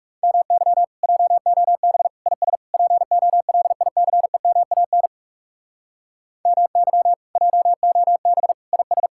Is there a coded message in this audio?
Yes: Morse code from 0.2 to 9.1 s, heard as a keyed tone, about 48 wpm, 700 Hz. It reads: MY JOB IS POLICEMAN MY JOB IS